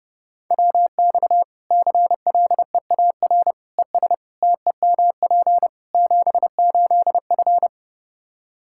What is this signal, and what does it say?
Morse code, heard as a keyed tone, about 30 wpm, 700 Hz. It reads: WX CLEAR ES TEMP 78F
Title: K1USN SST QSO Practice 30wpm